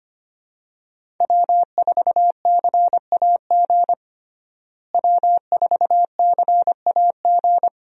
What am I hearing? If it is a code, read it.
Morse code, heard as a keyed tone, about 25 wpm, 700 Hz. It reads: W4CAG W4CAG
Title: USA Call Signs - 5 Characters - Repeated 2x - 25wpm